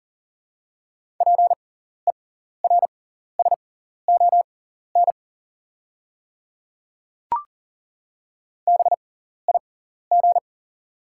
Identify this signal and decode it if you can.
Morse code, heard as a keyed tone, about 40 wpm, spaced as for 14 wpm, 690 Hz. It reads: PERSON BIG
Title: 6x Character Spacing - Top 500 Words - 40wpm